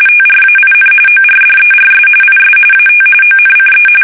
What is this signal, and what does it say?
RTTY, about 45.45 baud. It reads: TRAIN ORDERS IS AT ONCE
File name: fsk850